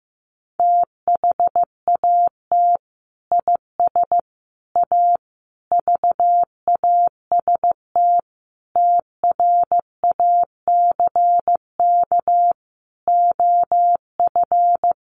Morse code, heard as a keyed tone, about 15 wpm, 700 Hz. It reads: THAT IS A VAST TRACK OF
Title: Sentences from Top 1300 Words 15wpm